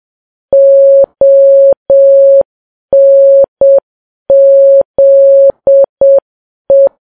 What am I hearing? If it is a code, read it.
Morse code, heard as a keyed tone, about 7 wpm, 550 Hz. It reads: ONZE
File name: Files Web tls / Morse on9fv / morse7kps